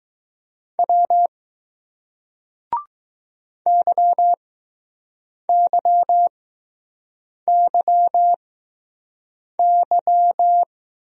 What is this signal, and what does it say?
Morse code, heard as a keyed tone, about 20 wpm, spaced as for 6 wpm, 700 Hz. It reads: W YYYY